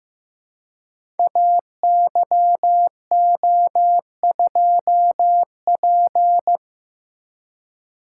Morse code, heard as a keyed tone, about 15 wpm, 700 Hz. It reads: AYO2P
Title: Code Groups - 5 Letter:Numbers - 15wpm